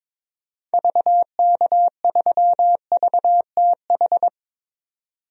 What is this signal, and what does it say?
Morse code, heard as a keyed tone, about 22 wpm, 700 Hz. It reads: VK3VTH
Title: DXCC-100 Call Signs - 6 Characters - 22wpm